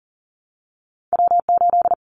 Morse code, heard as a keyed tone, about 40 wpm, 700 Hz. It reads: W8
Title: Callsign Prefixes for Top 25 DXCC - 40wpm